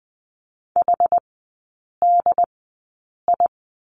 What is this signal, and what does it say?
Morse code, heard as a keyed tone, about 20 wpm, spaced as for 8 wpm, 700 Hz. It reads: HDI
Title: 081 Lesson 14e - Number 2 - Three Characters 20wpm